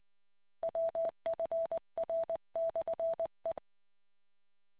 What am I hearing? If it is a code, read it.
Morse code, heard as a keyed tone, about 20 wpm, 660 Hz. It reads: WFR/E